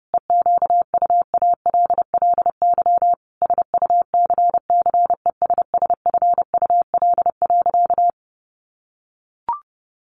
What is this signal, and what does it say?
Morse code, heard as a keyed tone, about 30 wpm, 700 Hz. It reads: EQUALLY SUCCESSFUL.
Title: Sentences from Top 1400 Words 30wpm